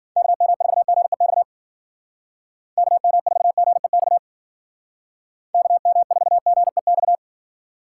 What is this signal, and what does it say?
Morse code, heard as a keyed tone, about 49 wpm, 700 Hz. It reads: KM4GEX KM4GEX KM4GEX
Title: Speed Racing - USA 2x3 Call Signs 40wpm